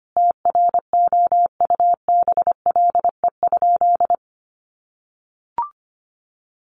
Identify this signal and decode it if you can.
Morse code, heard as a keyed tone, about 25 wpm, 700 Hz. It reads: TROUBLE?
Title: Sentences from Top 1200 Words 25wpm